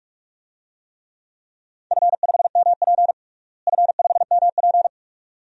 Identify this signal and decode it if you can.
Morse code, heard as a keyed tone, about 43 wpm, 700 Hz. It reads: F5MP F5MP
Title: DXCC-100 Call Signs - 4 Characters - Repeated 2x - 45wpm